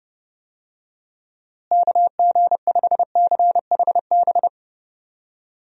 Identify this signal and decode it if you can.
Morse code, heard as a keyed tone, about 30 wpm, 700 Hz. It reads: KG5CHB
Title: USA Call Signs - 6 Characters - Repeated 2x - 30wpm